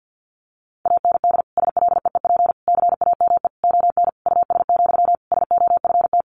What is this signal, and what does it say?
Morse code, heard as a keyed tone, about 50 wpm, 700 Hz. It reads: AND SLEEP CAME ON US, SOFT
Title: Sentences from Top 1500 Words 50wpm